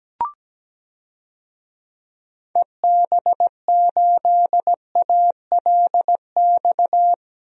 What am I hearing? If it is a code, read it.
Morse code, heard as a keyed tone, about 17 wpm, 700 Hz. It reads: EB8ALX
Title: DXCC-100 Call Signs - 6 Characters - 17wpm